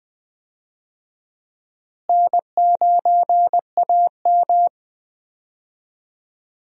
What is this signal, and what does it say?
Morse code, heard as a keyed tone, about 20 wpm, 700 Hz. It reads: N9AM
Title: USA Call Signs - 4 Characters - Repeated 2x - 20wpm